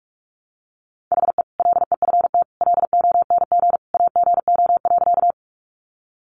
Morse code, heard as a keyed tone, about 47 wpm, 700 Hz. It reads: HE LEFT LONG AGO.
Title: Sentences from Top 500 Words 45wpm